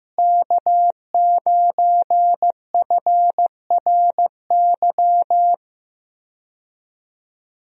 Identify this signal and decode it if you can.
Morse code, heard as a keyed tone, about 15 wpm, 700 Hz. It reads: K9FRY
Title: USA Call Signs - 5 Characters - Repeated 3x - 15wpm